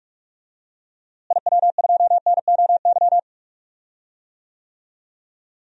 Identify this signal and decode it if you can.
Morse code, heard as a keyed tone, about 42 wpm, 700 Hz. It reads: IW2NOY
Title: DXCC-100 Call Signs - 6 Characters - Repeated 2x - 45wpm